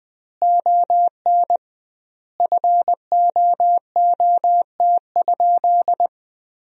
Morse code, heard as a keyed tone, about 20 wpm, 700 Hz. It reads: ON FOOT?
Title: 2x Word-Spacing - Sentences from Top 500 Words - 20wpm